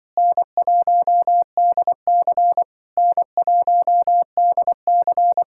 Morse code, heard as a keyed tone, about 24 wpm, 700 Hz. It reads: N1DC N1DC
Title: Speed-Racing Common QSO Phrases Encore - 20wpm